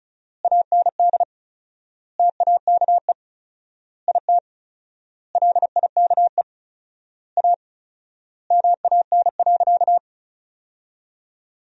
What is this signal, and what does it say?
Morse code, heard as a keyed tone, about 34 wpm, 700 Hz. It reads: AND TAKE IT LIKE A MAN.
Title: 4x Word-Spacing - Sentences from Top 300 Words - 35wpm